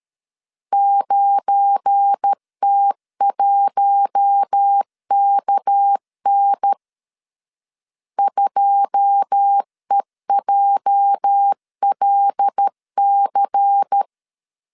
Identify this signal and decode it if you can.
Morse code, heard as a keyed tone, about 13 wpm, 800 Hz. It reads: 9T1KN 2EJLC